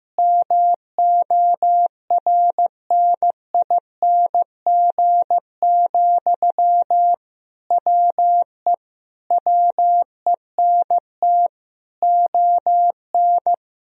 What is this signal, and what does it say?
Morse code, heard as a keyed tone, about 15 wpm, 700 Hz. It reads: MORNING, WE WENT ON
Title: Sentences from Top 1000 Words - Review Part 4 - 15wpm